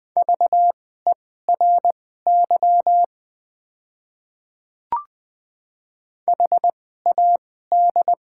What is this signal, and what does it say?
Morse code, heard as a keyed tone, about 21 wpm, spaced as for 15 wpm, 690 Hz. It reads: VERY HAD
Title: 2x Character Spacing - Top 500 Words - 20wpm